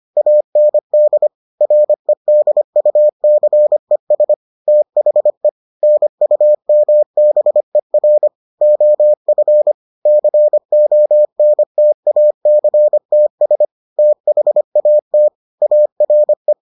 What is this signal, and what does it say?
Morse code, heard as a keyed tone, about 25 wpm, 600 Hz. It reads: AND REDUCES THE NUMBER OF CONTACTS THAT ARE